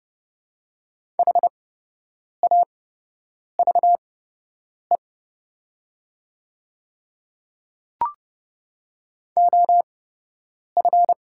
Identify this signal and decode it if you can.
Morse code, heard as a keyed tone, about 30 wpm, spaced as for 8 wpm, 700 Hz. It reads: HAVE OF
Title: 8x Character Spacing - Top 100 Words - 30wpm